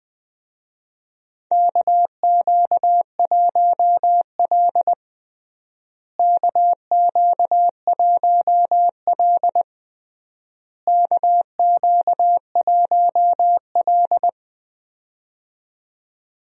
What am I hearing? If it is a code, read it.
Morse code, heard as a keyed tone, about 20 wpm, 700 Hz. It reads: KQ1L KQ1L KQ1L
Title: USA Call Signs - 4 Characters - Repeated 3x - 20wpm